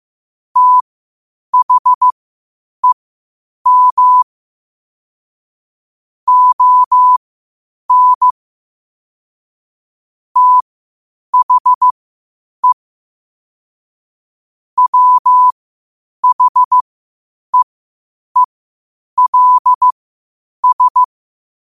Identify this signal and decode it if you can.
Morse code, heard as a keyed tone, about 15 wpm, spaced as for 8 wpm, 1000 Hz. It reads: THEM ON THE WHEELS
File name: qotdfiles / 8wpm